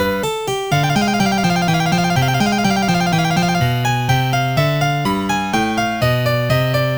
There is no code.